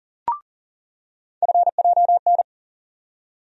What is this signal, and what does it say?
Morse code, heard as a keyed tone, about 40 wpm, 700 Hz. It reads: FJN
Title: USA Call Sign Postfixes - 3 Letters 40wpm